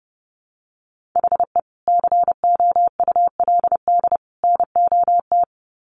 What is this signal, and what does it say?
Morse code, heard as a keyed tone, about 30 wpm, 700 Hz. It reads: HE COULD NOT
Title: Sets of 3 Words 30wpm